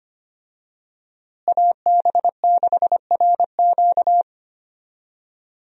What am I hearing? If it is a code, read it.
Morse code, heard as a keyed tone, about 25 wpm, 700 Hz. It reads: AB6RQ
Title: USA Call Signs - 5 Characters - Repeated 3x - 25wpm